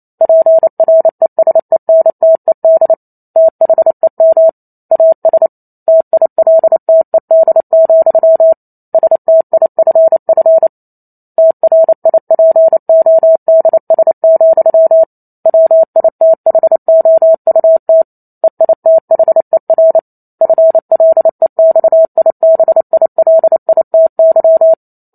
Morse code, heard as a keyed tone, about 28 wpm, 650 Hz. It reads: PRESENTED THEM AS TILTED, STIFF TRIPODS, WITHOUT EITHER FLEXIBILITY